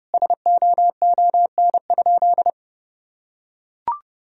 Morse code, heard as a keyed tone, about 30 wpm, 700 Hz. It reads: SOON?
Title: Sentences from Top 1000 Words - Review Part 4 - 30wpm